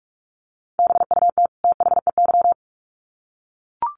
Morse code, heard as a keyed tone, about 44 wpm, 700 Hz. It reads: BUT THEY
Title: Sets of 2 Words 45wpm